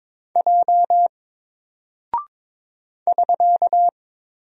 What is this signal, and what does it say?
Morse code, heard as a keyed tone, about 22 wpm, 700 Hz. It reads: J <SK>
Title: Single Character (Letter:Number:Punctuation:Prosign) Alt 22wpm